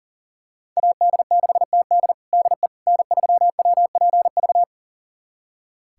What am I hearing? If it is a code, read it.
Morse code, heard as a keyed tone, about 37 wpm, 700 Hz. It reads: AD6TD DE N3WPV
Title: USA Callsign DE Callsign - 5 Characters - 40wpm